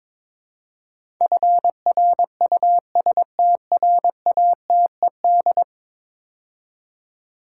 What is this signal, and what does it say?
Morse code, heard as a keyed tone, about 22 wpm, 700 Hz. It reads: FRUSTRATED
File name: practice / 22wpm / 10 Letter Words - First 3 Letters Given - 22wpm